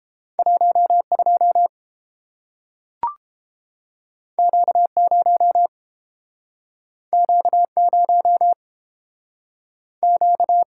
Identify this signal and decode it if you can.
Morse code, heard as a keyed tone, about 31 wpm, 700 Hz. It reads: 12 Q0 Q0 Q